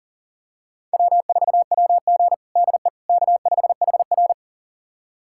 Morse code, heard as a keyed tone, about 40 wpm, 700 Hz. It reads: W4WG DE K5HR
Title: USA Callsign DE Callsign - 4 Characters - 40wpm